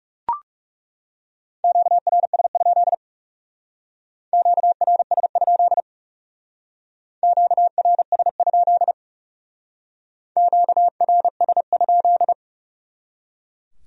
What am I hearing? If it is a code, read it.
Morse code, heard as a keyed tone, about 38 wpm, 700 Hz. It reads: QRS? QRS? QRS? QRS?